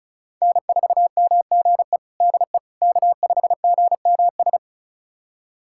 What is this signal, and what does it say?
Morse code, heard as a keyed tone, about 34 wpm, 700 Hz. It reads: N4MGE DE K5GMS
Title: USA Callsign DE Callsign - 5 Characters - 35wpm